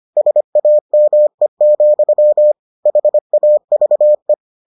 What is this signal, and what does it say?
Morse code, heard as a keyed tone, about 25 wpm, 600 Hz. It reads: SAME, HAVE